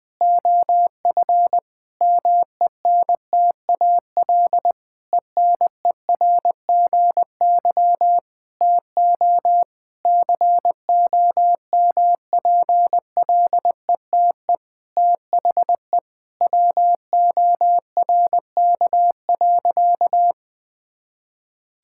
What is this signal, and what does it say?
Morse code, heard as a keyed tone, about 21 wpm, 700 Hz. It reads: OF MENTAL ENERGY TO COMPLETE THE WORK.
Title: Sentences from Top 1000 Words - Review Part 6 - 20wpm